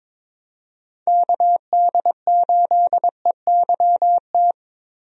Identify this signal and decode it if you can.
Morse code, heard as a keyed tone, about 22 wpm, 700 Hz. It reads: KD8EYT
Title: USA Call Signs - 6 Characters - Repeated 3x - 22wpm